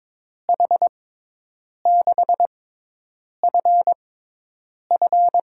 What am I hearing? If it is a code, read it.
Morse code, heard as a keyed tone, about 22 wpm, spaced as for 8 wpm, 700 Hz. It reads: H6FF
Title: USA Call Signs - 6 Characters - 6x Character Spacing - 22wpm